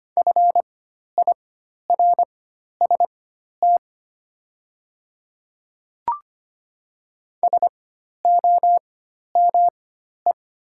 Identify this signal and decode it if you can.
Morse code, heard as a keyed tone, about 25 wpm, spaced as for 12 wpm, 700 Hz. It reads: FIRST SOME
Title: 4x Character Spacing - Top 100 Words - 25wpm